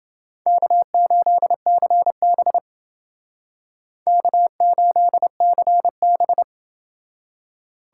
Morse code, heard as a keyed tone, about 29 wpm, 700 Hz. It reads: K8CB K8CB